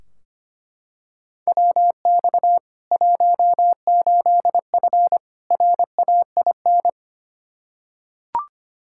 Morse code, heard as a keyed tone, about 25 wpm, 700 Hz. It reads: WX 18F RAIN
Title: Practice QSO - Exchange 2 - Info - 25wpm